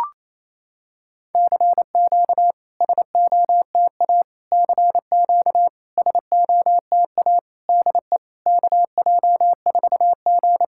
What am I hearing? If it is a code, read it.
Morse code, heard as a keyed tone, about 28 wpm, 690 Hz. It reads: CQ SOTA CQ SOTA DE KJ4G
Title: SOTA QSO Practice 28wpm